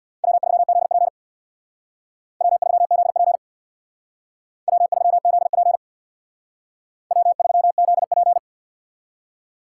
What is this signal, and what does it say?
Morse code, heard as a keyed tone, about 64 wpm, 700 Hz. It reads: W3ZP W3ZP W3ZP W3ZP